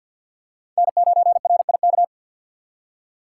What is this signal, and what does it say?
Morse code, heard as a keyed tone, about 51 wpm, 700 Hz. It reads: N9RIK
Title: USA Call Signs - 3x Character Spacing - 50wpm